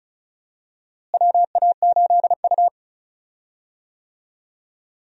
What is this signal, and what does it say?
Morse code, heard as a keyed tone, about 34 wpm, 700 Hz. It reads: WA8U
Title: USA Call Signs - 4 Characters - Repeated 2x - 35wpm